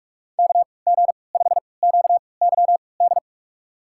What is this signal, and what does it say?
Morse code, heard as a keyed tone, about 46 wpm, spaced as for 26 wpm, 700 Hz. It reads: KG5QYD
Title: USA Call Signs - 6 Characters - 3x Character Spacing - 45wpm